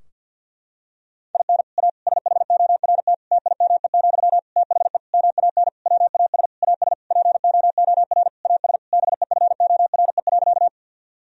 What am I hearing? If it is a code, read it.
Morse code, heard as a keyed tone, about 52 wpm, 700 Hz. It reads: IN A SHORT TIME, THE MAN WAS AS POOR AS BEFORE.
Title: Sentences from Top 1000 Words - Review Part 2 - 50wpm